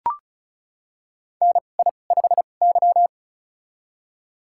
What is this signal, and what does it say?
Morse code, heard as a keyed tone, about 36 wpm, spaced as for 25 wpm, 700 Hz. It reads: NI5Y
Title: USA Call Signs - 4 Characters - 2x Character Spacing - 35wpm